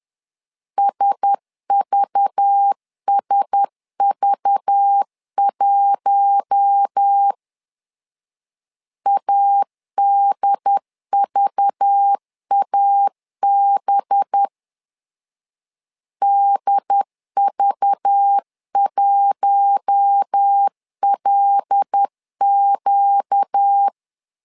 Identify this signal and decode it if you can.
Morse code, heard as a keyed tone, about 10 wpm, 800 Hz. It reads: SVSV1 ADVAB DV1LQ
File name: LEZIONI / 5 LBVC / 5 LBV 10wpm 2248